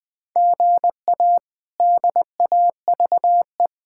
Morse code, heard as a keyed tone, about 20 wpm, 700 Hz. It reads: GA DAVE
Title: K1USN SST QSO Practice - 20wpm